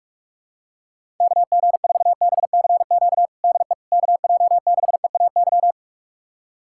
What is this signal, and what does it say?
Morse code, heard as a keyed tone, about 47 wpm, 700 Hz. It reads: KG4BCQ DE KJ6EAY
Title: USA Callsign DE Callsign - 6 Characters - 45wpm